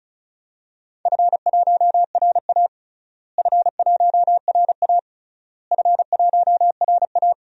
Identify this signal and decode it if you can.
Morse code, heard as a keyed tone, about 36 wpm, 700 Hz. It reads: F1RA F1RA F1RA
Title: DXCC-100 Call Signs - 4 Characters - Repeated 3x - 35wpm